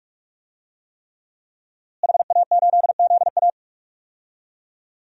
Morse code, heard as a keyed tone, about 42 wpm, 700 Hz. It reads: HA8ZA
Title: DXCC-100 Call Signs - 5 Characters - Repeated 3x - 45wpm